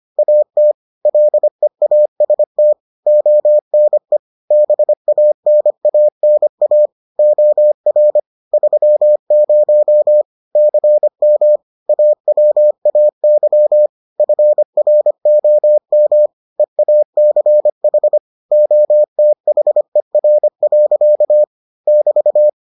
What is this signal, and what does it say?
Morse code, heard as a keyed tone, about 25 wpm, 600 Hz. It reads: AT LEAST ONE BANANA OR 30 CM AWAY FROM EACH OTHER. <BT>